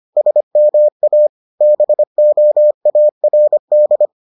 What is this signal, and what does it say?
Morse code, heard as a keyed tone, about 25 wpm, 600 Hz. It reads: SMA BOARD